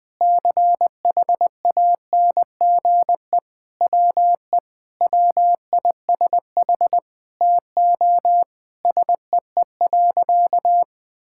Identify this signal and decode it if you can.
Morse code, heard as a keyed tone, about 20 wpm, 700 Hz. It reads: CHANGE WE WISH TO SEE.
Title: Sentences with 8-Word Phrases - 20wpm